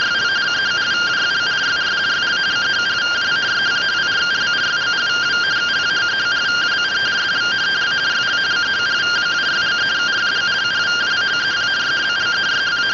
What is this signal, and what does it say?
Two-tone teleprinter signal, about 170 Hz shift.